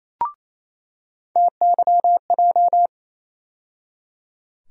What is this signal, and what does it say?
Morse code, heard as a keyed tone, about 28 wpm, 700 Hz. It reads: TYJ
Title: USA Call Sign Postfixes - 3 Letters 28wpm